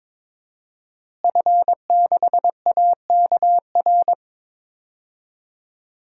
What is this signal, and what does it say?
Morse code, heard as a keyed tone, about 21 wpm, 700 Hz. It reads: F6AKR
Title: DXCC-100 Call Signs - 5 Characters - Repeated 2x - 22wpm